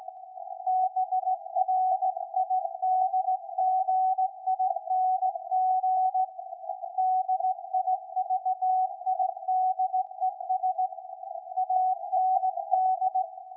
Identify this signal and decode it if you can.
Morse code, heard as a keyed tone, about 17 wpm, 750 Hz. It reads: BRIDGING DIVIDES AND
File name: iz7slz / VLF / SAQ 13-02-2023